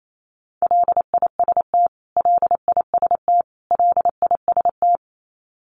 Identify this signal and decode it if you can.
Morse code, heard as a keyed tone, about 28 wpm, 700 Hz. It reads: LIST LIST LIST
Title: Top 500 Words Repeated 28wpm